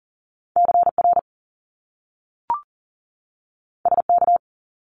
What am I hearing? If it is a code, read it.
Morse code, heard as a keyed tone, about 42 wpm, 700 Hz. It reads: CR SK
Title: Top Two Letter Combinations in Words 40wpm